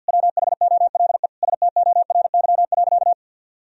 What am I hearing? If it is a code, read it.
Morse code, heard as a keyed tone, about 52 wpm, 700 Hz. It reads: WHOLE STORY.
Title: Sentences from Top 1000 Words - Review Part 3 - 50wpm